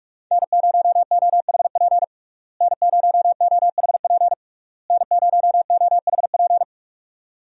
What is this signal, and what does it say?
Morse code, heard as a keyed tone, about 46 wpm, 700 Hz. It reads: N0OHP N0OHP N0OHP
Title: USA Call Signs - 5 Characters - Repeated 3x - 45wpm